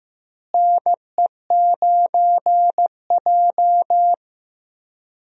Morse code, heard as a keyed tone, about 15 wpm, 700 Hz. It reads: NE9J
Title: USA Call Signs - 4 Characters - Repeated 3x - 15wpm